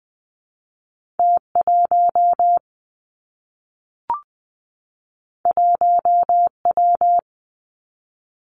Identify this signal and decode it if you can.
Morse code, heard as a keyed tone, about 20 wpm, 700 Hz. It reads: T 1 1 W
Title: Two Challenging Letter-Number Combinations 20wpm